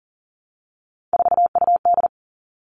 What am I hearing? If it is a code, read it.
Morse code, heard as a keyed tone, about 40 wpm, 700 Hz. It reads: 4UD
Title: All Cut Numbers - 3-Digit 40wpm